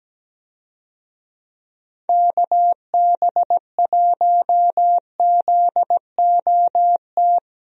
Morse code, heard as a keyed tone, about 17 wpm, 700 Hz. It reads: KB1ZOT
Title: USA Call Signs - 6 Characters - Repeated 3x - 17wpm